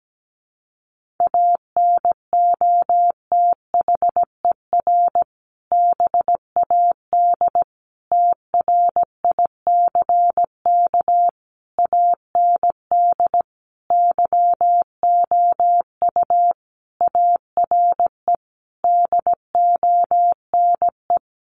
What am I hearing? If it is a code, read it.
Morse code, heard as a keyed tone, about 17 wpm, 700 Hz. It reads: ANOTHER BAD TRICK AND YOU ARE DONE
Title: Sentences from Top 1500 Words 17wpm